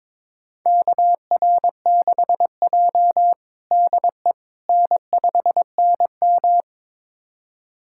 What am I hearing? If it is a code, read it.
Morse code, heard as a keyed tone, about 22 wpm, 700 Hz. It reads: KR6J DE N5NM
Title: USA Callsign DE Callsign - 4 Characters - 22wpm